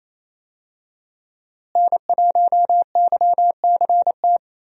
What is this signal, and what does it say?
Morse code, heard as a keyed tone, about 28 wpm, 700 Hz. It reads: N1YCT
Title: USA Call Signs - 5 Characters - Repeated 3x - 28wpm